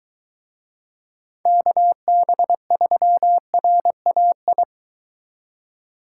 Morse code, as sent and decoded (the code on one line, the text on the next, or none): -.- -... ...-- .-. .- ..
KB3RAI